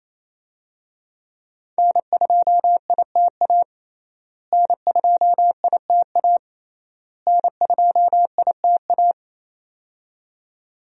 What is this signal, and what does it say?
Morse code, heard as a keyed tone, about 28 wpm, 700 Hz. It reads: N2ITA N2ITA N2ITA
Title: USA Call Signs - 5 Characters - Repeated 3x - 28wpm